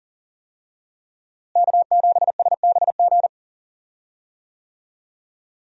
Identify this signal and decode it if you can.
Morse code, heard as a keyed tone, about 40 wpm, 700 Hz. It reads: K7SBG
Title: USA Call Signs - 5 Characters - Repeated 2x - 40wpm